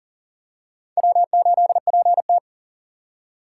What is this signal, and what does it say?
Morse code, heard as a keyed tone, about 40 wpm, 700 Hz. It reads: W8PT
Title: USA Call Signs - 4x Character Spacing - 40wpm